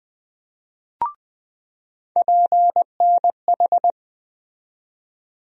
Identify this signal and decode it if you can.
Morse code, heard as a keyed tone, about 20 wpm, 700 Hz. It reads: PNH
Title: USA Call Sign Postfixes - 3 Letters 20wpm